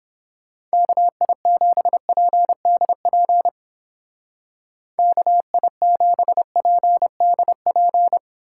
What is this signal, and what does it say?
Morse code, heard as a keyed tone, about 31 wpm, 700 Hz. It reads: KI7PDP KI7PDP